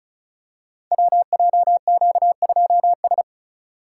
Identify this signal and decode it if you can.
Morse code, heard as a keyed tone, about 36 wpm, 700 Hz. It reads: WJQ2S
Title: Code Groups - 5 Letter:Numbers - 35wpm